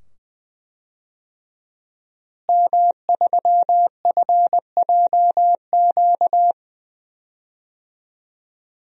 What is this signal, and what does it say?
Morse code, heard as a keyed tone, about 21 wpm, 700 Hz. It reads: M3FJQ
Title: DXCC-100 Call Signs - 5 Characters - Repeated 2x - 20wpm